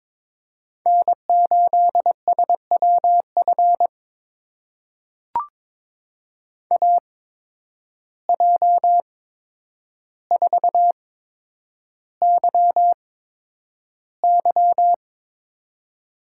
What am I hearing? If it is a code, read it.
Morse code, heard as a keyed tone, about 23 wpm, spaced as for 10 wpm, 700 Hz. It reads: N8SWF AJ4YY
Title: USA Call Signs - 5 Characters - 8x Character Spacing - 22wpm